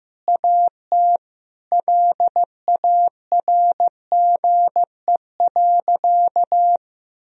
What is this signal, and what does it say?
Morse code, heard as a keyed tone, about 15 wpm, 700 Hz. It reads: AT LARGE.
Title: Sentences from Top 1000 Words - Review Part 1 - 15wpm